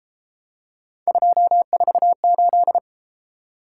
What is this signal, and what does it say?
Morse code, heard as a keyed tone, about 30 wpm, 700 Hz. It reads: 248